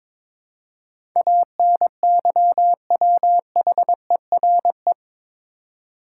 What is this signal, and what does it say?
Morse code, heard as a keyed tone, about 22 wpm, 700 Hz. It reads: ANYWHERE
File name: practice / 22wpm / Eight Letter Words - 22wpm